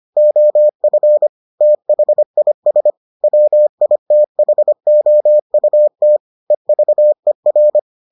Morse code, heard as a keyed tone, about 25 wpm, 600 Hz. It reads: OF THIS WITHOUT EVER